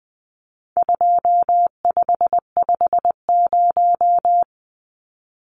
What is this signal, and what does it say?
Morse code, heard as a keyed tone, about 20 wpm, 700 Hz. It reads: 2550
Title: Numbers - 4-Digit 20wpm